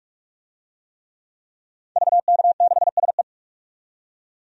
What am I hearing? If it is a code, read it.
Morse code, heard as a keyed tone, about 46 wpm, 700 Hz. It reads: VK6SE
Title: DXCC-100 Call Signs - 5 Characters - 45wpm